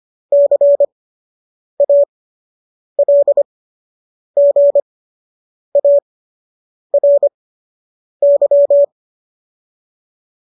Morse code, heard as a keyed tone, about 25 wpm, spaced as for 8 wpm, 570 Hz. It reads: CALGARY